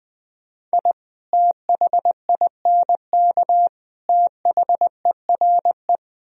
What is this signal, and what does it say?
Morse code, heard as a keyed tone, about 20 wpm, 700 Hz. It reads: I THINK THERE
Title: Speed-Racing - Sets of 3 Words 20wpm